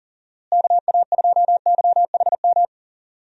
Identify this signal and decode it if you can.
Morse code, heard as a keyed tone, about 40 wpm, 700 Hz. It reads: KA2YHM
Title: USA Call Signs - 6 Characters - Repeated 3x - 40wpm